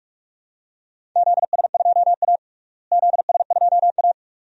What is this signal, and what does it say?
Morse code, heard as a keyed tone, about 46 wpm, 700 Hz. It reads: ZS2A ZS2A
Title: DXCC-100 Call Signs - 4 Characters - Repeated 2x - 45wpm